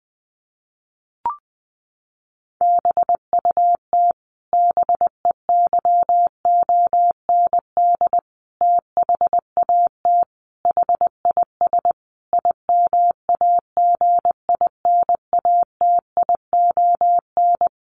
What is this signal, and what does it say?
Morse code, heard as a keyed tone, about 20 wpm, 700 Hz. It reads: BUT BEYOND THAT HIS IMAGINATION
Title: Sentences from Top 1300 Words 20wpm